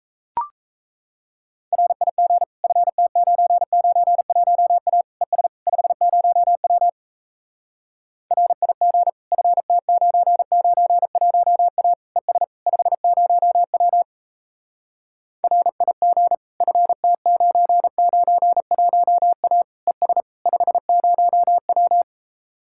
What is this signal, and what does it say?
Morse code, heard as a keyed tone, about 37 wpm, 700 Hz. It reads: RIG FT991A ES 50W RIG FT991A ES 50W RIG FT991A ES 50W